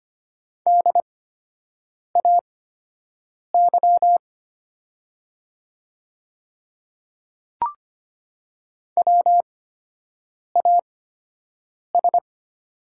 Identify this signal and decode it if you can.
Morse code, heard as a keyed tone, about 25 wpm, spaced as for 7 wpm, 700 Hz. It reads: DAY WAS
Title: 8x Character Spacing - Top 100 Words - 25wpm